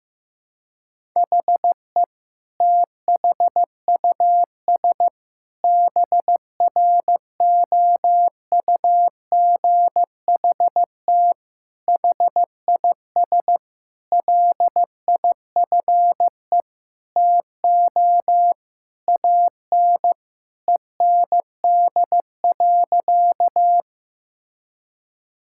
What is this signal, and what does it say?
Morse code, heard as a keyed tone, about 16 wpm, 700 Hz. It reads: HE THUS BROUGHT HIS LIFE TO AN END.
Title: Sentences from Top 1000 Words - Review Part 5 - 15wpm